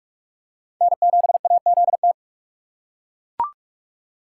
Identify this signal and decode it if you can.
Morse code, heard as a keyed tone, about 39 wpm, 700 Hz. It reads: N7AZT